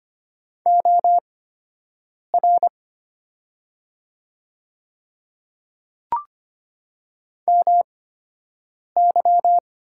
Morse code, heard as a keyed tone, about 25 wpm, spaced as for 7 wpm, 700 Hz. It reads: OR MY
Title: 8x Character Spacing - Top 100 Words - 25wpm